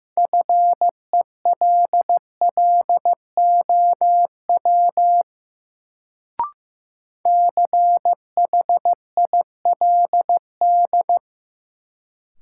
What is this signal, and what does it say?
Morse code, heard as a keyed tone, about 16 wpm, 700 Hz. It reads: FELLOW CHILD
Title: Sets of 5 Words Encore - New Word Drill - 15wpm